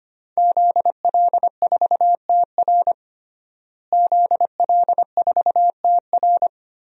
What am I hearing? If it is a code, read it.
Morse code, heard as a keyed tone, about 25 wpm, 700 Hz. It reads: ZL4TR ZL4TR
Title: DXCC-100 Call Signs - 5 Characters - Repeated 2x - 25wpm